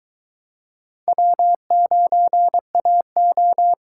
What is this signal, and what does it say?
Morse code, heard as a keyed tone, about 23 wpm, 700 Hz. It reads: W9AO